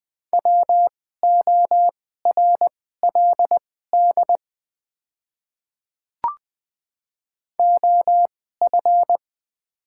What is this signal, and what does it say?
Morse code, heard as a keyed tone, about 20 wpm, spaced as for 14 wpm, 690 Hz. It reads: WORLD OF
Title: 2x Character Spacing - Top 200 Words - 20wpm